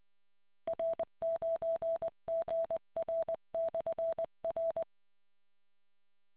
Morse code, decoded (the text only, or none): R9GR/R